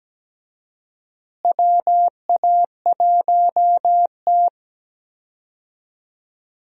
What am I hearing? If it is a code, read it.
Morse code, heard as a keyed tone, about 17 wpm, 700 Hz. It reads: WA1T
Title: USA Call Signs - 4 Characters - Repeated 3x - 17wpm